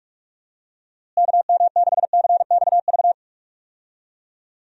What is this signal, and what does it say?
Morse code, heard as a keyed tone, about 44 wpm, 700 Hz. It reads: KM6CXV
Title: USA Call Signs - 6 Characters - Repeated 3x - 45wpm